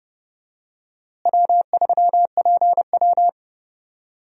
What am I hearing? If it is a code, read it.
Morse code, heard as a keyed tone, about 32 wpm, 700 Hz. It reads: W3PW
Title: USA Call Signs - 4 Characters - Repeated 3x - 30wpm